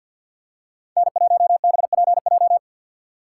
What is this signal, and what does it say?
Morse code, heard as a keyed tone, about 52 wpm, 690 Hz. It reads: N1BPJ
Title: USA Call Signs - 5 Characters - 6x Character Spacing - 50wpm